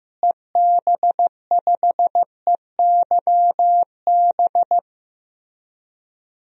Morse code, heard as a keyed tone, about 15 wpm, 700 Hz. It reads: EB5EYB
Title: DXCC-100 Call Signs - 6 Characters - Repeated 2x - 15wpm